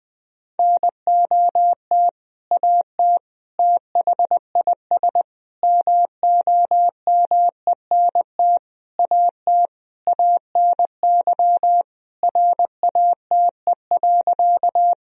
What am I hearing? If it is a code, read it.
Morse code, heard as a keyed tone, about 21 wpm, 700 Hz. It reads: NOT AT THIS MOMENT AT ANY RATE.
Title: Sentences from Top 1000 Words - Review Part 4 - 20wpm